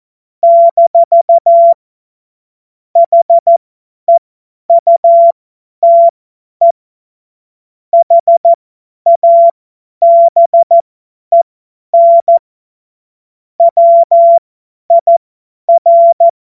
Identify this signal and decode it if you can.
Morse code, heard as a keyed tone, about 14 wpm, spaced as for 10 wpm, 680 Hz. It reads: - HEUTE HABEN WIR
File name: Morsekurs / Witze / Witz 26 März 2024 50 F70